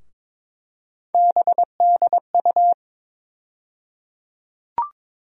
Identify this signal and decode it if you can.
Morse code, heard as a keyed tone, about 22 wpm, 700 Hz. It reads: BDU
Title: USA Call Sign Postfixes - 3 Letters 22wpm